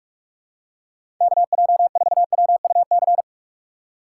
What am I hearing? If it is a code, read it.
Morse code, heard as a keyed tone, about 44 wpm, 700 Hz. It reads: KJ4WUC
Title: USA Call Signs - 4x Character Spacing - 45wpm